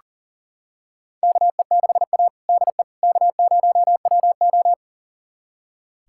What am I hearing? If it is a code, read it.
Morse code, heard as a keyed tone, about 39 wpm, 700 Hz. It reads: KE6A DE K0WO
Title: USA Callsign DE Callsign - 4 Characters - 40wpm